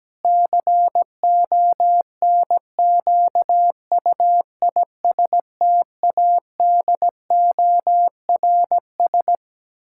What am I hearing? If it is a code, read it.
Morse code, heard as a keyed tone, about 17 wpm, 700 Hz. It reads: CONQUISTADORS
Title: Pangram Challenge v2 - 17wpm